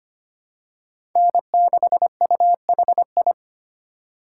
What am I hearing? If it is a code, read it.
Morse code, heard as a keyed tone, about 25 wpm, 700 Hz. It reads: N6UHI
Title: USA Call Signs - 5 Characters - 6x Character Spacing - 25wpm